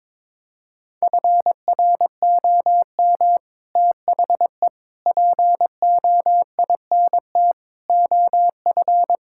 Morse code, heard as a keyed tone, about 22 wpm, 700 Hz. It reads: FROM THE POINT OF
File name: practice / 22wpm / Build A Phrase - 5-Word Phrases - 22wpm